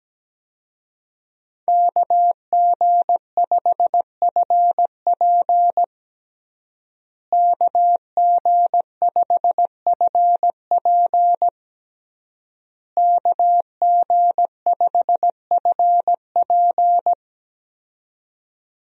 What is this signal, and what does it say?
Morse code, heard as a keyed tone, about 17 wpm, 700 Hz. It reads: KG5FP KG5FP KG5FP
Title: USA Call Signs - 5 Characters - Repeated 3x - 17wpm